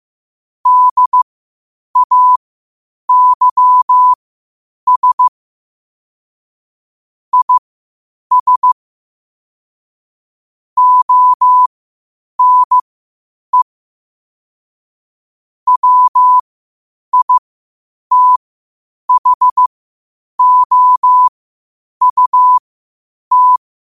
Morse code, heard as a keyed tone, about 15 wpm, spaced as for 8 wpm, 1000 Hz. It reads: DAYS IS ONE WITHOUT
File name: qotdfiles / 8wpm